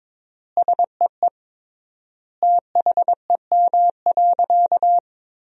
Morse code, heard as a keyed tone, about 22 wpm, 700 Hz. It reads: SEE THEM.
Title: 3x Word-Spacing - Sentences from Top 300 Words - 22wpm